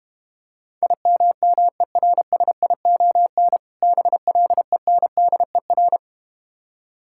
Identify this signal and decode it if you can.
Morse code, heard as a keyed tone, about 32 wpm, 700 Hz. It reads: IMMERSION BLENDER